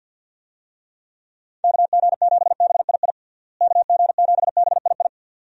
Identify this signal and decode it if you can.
Morse code, heard as a keyed tone, about 52 wpm, 700 Hz. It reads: KG7BII KG7BII
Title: USA Call Signs - 6 Characters - Repeated 2x - 50wpm